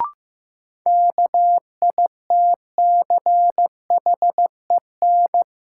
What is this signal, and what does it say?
Morse code, heard as a keyed tone, about 15 wpm, 700 Hz. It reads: KITCHEN
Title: Sentences from Top 1400 Words - New Word Drill - 15wpm